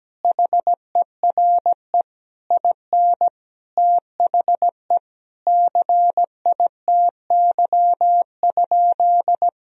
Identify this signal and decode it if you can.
Morse code, heard as a keyed tone, about 17 wpm, 700 Hz. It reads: HERE IN THE CITY?
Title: Sentences from Top 1000 Words - Review Part 1 - 17wpm